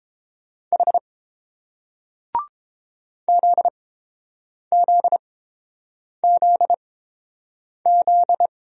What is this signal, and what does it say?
Morse code, heard as a keyed tone, about 30 wpm, spaced as for 7 wpm, 690 Hz. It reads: H ZZZZ